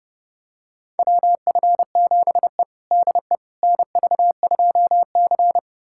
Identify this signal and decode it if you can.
Morse code, heard as a keyed tone, about 30 wpm, 700 Hz. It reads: WF7E DE NV2C
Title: USA Callsign DE Callsign - 4 Characters - 30wpm